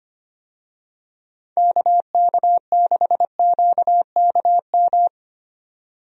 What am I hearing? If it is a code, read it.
Morse code, heard as a keyed tone, about 25 wpm, 700 Hz. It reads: KK6QKM
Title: USA Call Signs - 6 Characters - 25wpm